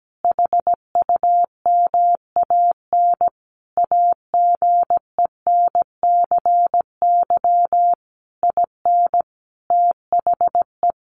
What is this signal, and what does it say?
Morse code, heard as a keyed tone, about 17 wpm, 700 Hz. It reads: HUMAN AGENCY IN THE
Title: Sentences from Top 1300 Words 17wpm